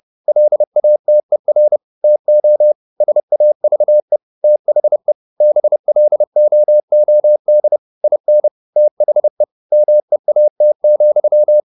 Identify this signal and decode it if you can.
Morse code, heard as a keyed tone, about 30 wpm, 600 Hz. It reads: LATER TO SAVE THE BLOOD IN THE MEAT,